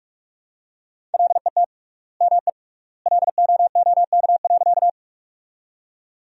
Morse code, heard as a keyed tone, about 44 wpm, 700 Hz. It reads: LET ME LOOK.
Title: 3x Word-Spacing - Sentences from Top 300 Words - 45wpm